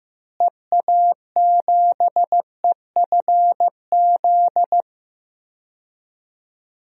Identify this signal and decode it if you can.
Morse code, heard as a keyed tone, about 15 wpm, 700 Hz. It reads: EA7EFZ
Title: DXCC-100 Call Signs - 6 Characters - Repeated 2x - 15wpm